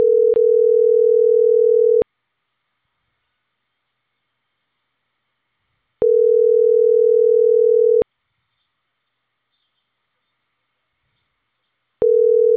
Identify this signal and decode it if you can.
Ringback tone (North America), 440 + 480 Hz.